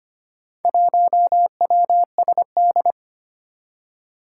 Morse code, heard as a keyed tone, about 25 wpm, 700 Hz. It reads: 1WSD
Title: Four Challenging Letter-Number Combinations 25wpm